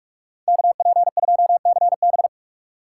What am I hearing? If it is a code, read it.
Morse code, heard as a keyed tone, about 41 wpm, 700 Hz. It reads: KP2CB